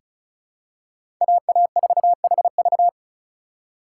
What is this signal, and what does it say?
Morse code, heard as a keyed tone, about 34 wpm, 700 Hz. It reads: AA4HV
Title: K1USN SST QSO Practice - 35wpm